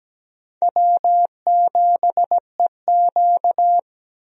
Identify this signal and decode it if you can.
Morse code, heard as a keyed tone, about 17 wpm, 700 Hz. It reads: W7EQ
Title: USA Call Signs - 4 Characters - 17wpm